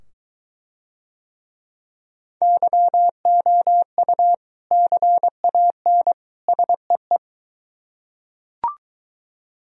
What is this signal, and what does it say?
Morse code, heard as a keyed tone, about 21 wpm, 700 Hz. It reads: YOU CAN SEE